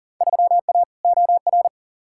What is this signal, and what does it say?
Morse code, heard as a keyed tone, about 40 wpm, 700 Hz. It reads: 3A OR
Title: Field Day Exchanges - 40wpm